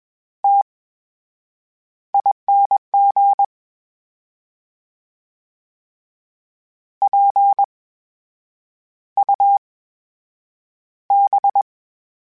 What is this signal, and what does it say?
Morse code, heard as a keyed tone, about 22 wpm, spaced as for 8 wpm, 800 Hz. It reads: TING PUB
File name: Special INGs-mp3s / Special INGs-mp3s / VerySlowToFaster Ing @20